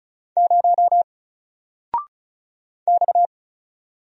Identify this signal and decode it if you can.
Morse code, heard as a keyed tone, about 36 wpm, 700 Hz. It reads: 0 X
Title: Single Letter-Number Alt 35wpm